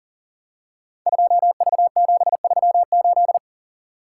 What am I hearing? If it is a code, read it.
Morse code, heard as a keyed tone, about 39 wpm, 690 Hz. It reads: 2V738
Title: Code Groups - 5 Letter:Numbers - 40wpm